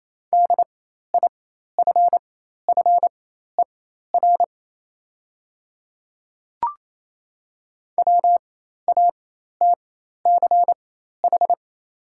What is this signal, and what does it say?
Morse code, heard as a keyed tone, about 28 wpm, spaced as for 13 wpm, 690 Hz. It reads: DIFFER WATCH
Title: 4x Character Spacing - Top 300 Words - 28wpm